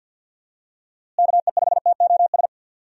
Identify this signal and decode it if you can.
Morse code, heard as a keyed tone, about 52 wpm, 700 Hz. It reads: KE5TOS
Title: USA Call Signs - 6 Characters - Repeated 2x - 50wpm